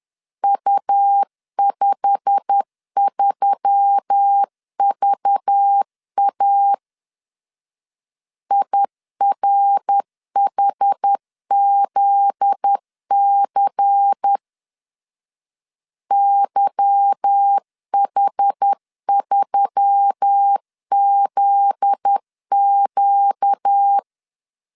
Morse code, heard as a keyed tone, about 10 wpm, 800 Hz. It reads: U53VA IRHZC YH3ZQ
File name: LEZIONI / 8 YJ45 / 8 YJ45 10wpm 2549